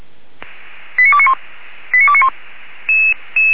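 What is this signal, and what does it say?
Five-tone selective call (ZVEI), heard as quick runs of five tones, tones of about 70 ms. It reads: 89291 89291
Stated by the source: ZVEI five-tone selective call received with an SDR, recorded by feuerwehr-orlamuende fire brigade.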